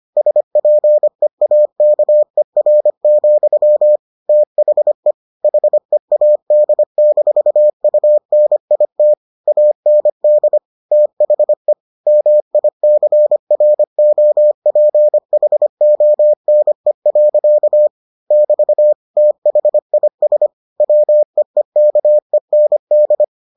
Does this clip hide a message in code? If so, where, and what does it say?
Yes: Morse code from 0.2 to 23.2 s, heard as a keyed tone, about 25 wpm, 600 Hz. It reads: SPEAKER, THE HEAD-UNIT AND THE MICROPHONE. <BT> THIS WEEKEND